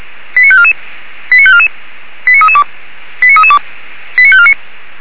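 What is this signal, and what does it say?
Five-tone selective call (ZVEI), heard as quick runs of five tones, tones of about 70 ms. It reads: 89544 89544 89332 89332 89549
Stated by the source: ZVEI five-tone selective call received via SDR, from a fire brigade in Orlamünde.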